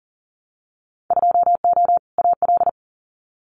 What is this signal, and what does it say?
Morse code, heard as a keyed tone, about 39 wpm, 700 Hz. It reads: 2O AL
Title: Winter Field Day Exchanges Encore 40wpm